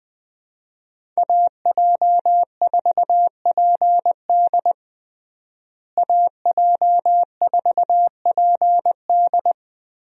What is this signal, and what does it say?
Morse code, heard as a keyed tone, about 21 wpm, 700 Hz. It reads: AJ4PD AJ4PD
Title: USA Call Signs - 5 Characters - Repeated 2x - 20wpm